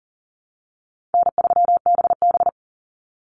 Morse code, heard as a keyed tone, about 40 wpm, 700 Hz. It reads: N3BB
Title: Easy Call Signs Repeated 2x - 40wpm